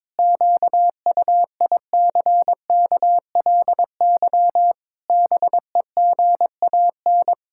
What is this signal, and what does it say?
Morse code, heard as a keyed tone, about 22 wpm, 700 Hz. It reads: QUICKLY BEGAN
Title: Pangram Challenge v2 - 22wpm